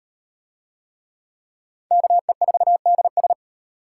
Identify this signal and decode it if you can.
Morse code, heard as a keyed tone, about 36 wpm, 700 Hz. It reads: KE4DS